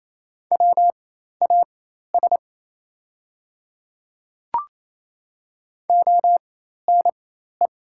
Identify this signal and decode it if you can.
Morse code, heard as a keyed tone, about 27 wpm, spaced as for 13 wpm, 690 Hz. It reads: WAS ONE
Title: 4x Character Spacing - Top 100 Words - 28wpm